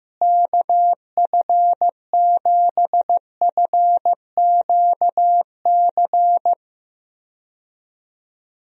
Morse code, heard as a keyed tone, about 15 wpm, 700 Hz. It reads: KF7FQC
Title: USA Call Signs - 6 Characters - Repeated 2x - 15wpm